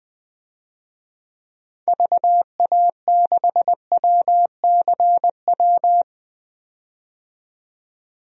Morse code, heard as a keyed tone, about 21 wpm, 700 Hz. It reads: VA6WCW
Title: DXCC-100 Call Signs - 6 Characters - 20wpm